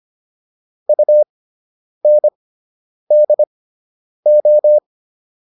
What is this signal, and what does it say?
Morse code, heard as a keyed tone, about 25 wpm, spaced as for 9 wpm, 600 Hz. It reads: UNDO